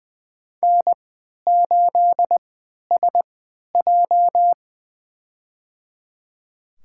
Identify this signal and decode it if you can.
Morse code, heard as a keyed tone, about 20 wpm, spaced as for 11 wpm, 700 Hz. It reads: N8SJ
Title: USA Call Signs - 3x Character Spacing - 20wpm